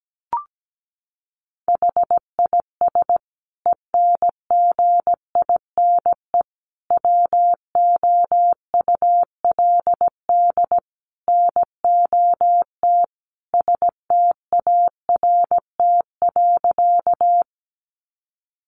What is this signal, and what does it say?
Morse code, heard as a keyed tone, about 17 wpm, 700 Hz. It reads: HIS ENGINE WOULD NOT START.
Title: Sentences from Top 1500 Words 17wpm